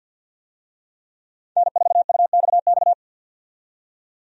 Morse code, heard as a keyed tone, about 50 wpm, 700 Hz. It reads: N4UXX
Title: USA Call Signs - 5 Characters - 50wpm